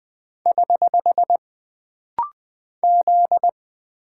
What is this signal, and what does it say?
Morse code, heard as a keyed tone, about 20 wpm, 700 Hz. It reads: <HH> Z
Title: Single Character (Letter:Number:Punctuation:Prosign) - Rapid-Fire - 20wpm